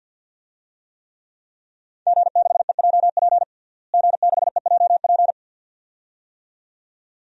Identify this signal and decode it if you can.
Morse code, heard as a keyed tone, about 50 wpm, 700 Hz. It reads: G6EJP G6EJP
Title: DXCC-100 Call Signs - 5 Characters - Repeated 2x - 50wpm